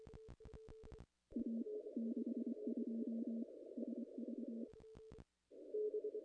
Morse code, heard as a keyed tone, about 24 wpm, 240 Hz. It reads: A 62 SV
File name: contests / ss-cw-k5tr-op-n5rz -0000